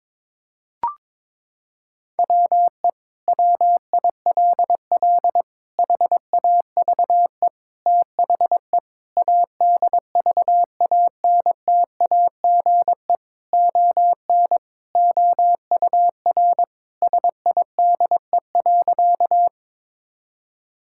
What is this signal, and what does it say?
Morse code, heard as a keyed tone, about 22 wpm, 700 Hz. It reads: WE WILL HAVE THE ADVANTAGE ON OUR SIDE.
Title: Sentences from Top 1000 Words - Review Part 6 - 22wpm